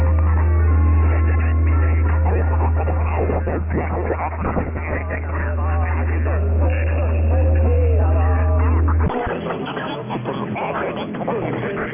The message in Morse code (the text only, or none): none